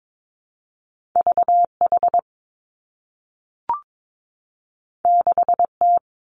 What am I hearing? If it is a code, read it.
Morse code, heard as a keyed tone, about 22 wpm, 700 Hz. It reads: V H 6 T
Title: Two Challenging Letter-Number Combinations 22wpm